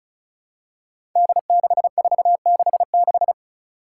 Morse code, heard as a keyed tone, about 34 wpm, 700 Hz. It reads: D6466
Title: Five Challenging Letter-Number Combinations 35wpm